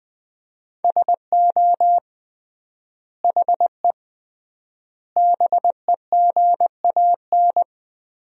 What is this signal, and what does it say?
Morse code, heard as a keyed tone, about 20 wpm, 700 Hz. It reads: SO HE BEGAN
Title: 3x Word-Spacing - Sentences from Top 300 Words - 20wpm